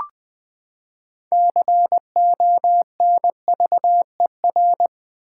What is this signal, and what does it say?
Morse code, heard as a keyed tone, about 20 wpm, 700 Hz. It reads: CONVER
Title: Build A Word - 6 Letters At A Time - 12-Letter Words - 20wpm